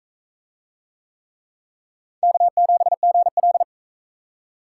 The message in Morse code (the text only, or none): K7GL